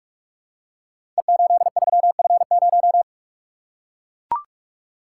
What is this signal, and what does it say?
Morse code, heard as a keyed tone, about 44 wpm, 700 Hz. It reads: E83F0
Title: Code Groups - 5 Letter:Numbers - 45wpm